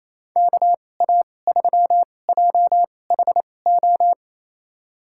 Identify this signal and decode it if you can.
Morse code, heard as a keyed tone, about 27 wpm, spaced as for 20 wpm, 700 Hz. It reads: KA3JHO
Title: USA Call Signs - 2x Character Spacing - 28wpm